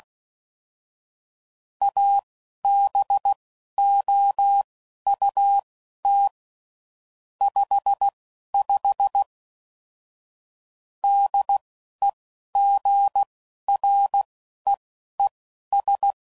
Morse code, heard as a keyed tone, about 16 wpm, spaced as for 11 wpm, 800 Hz. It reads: ABOUT 55 DEGREES
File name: CW Code Groups @10WPM / Weather Sets / Weather 4 24